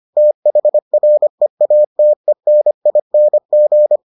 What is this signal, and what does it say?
Morse code, heard as a keyed tone, about 25 wpm, 600 Hz. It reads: THREATENING